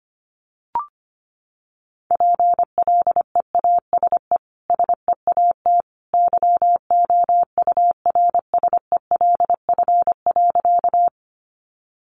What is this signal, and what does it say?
Morse code, heard as a keyed tone, about 25 wpm, 700 Hz. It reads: PLEASE SEAT YOURSELF.
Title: Sentences from Top 1500 Words 25wpm